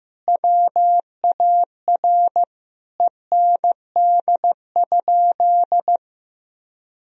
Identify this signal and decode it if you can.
Morse code, heard as a keyed tone, about 15 wpm, 700 Hz. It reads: WAR END?
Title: Sentences from Top 1000 Words - Review Part 2 - 15wpm